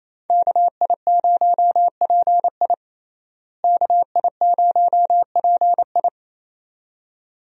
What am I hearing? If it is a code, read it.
Morse code, heard as a keyed tone, about 28 wpm, 700 Hz. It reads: KI0PI KI0PI
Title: USA Call Signs - 5 Characters - Repeated 2x - 28wpm